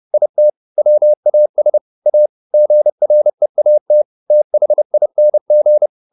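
Morse code, heard as a keyed tone, about 30 wpm, 600 Hz. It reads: IT WAS A GREAT THING
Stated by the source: Morse code at 30 wpm, the novel Never by Ken Follett in English, from CW med Gnister.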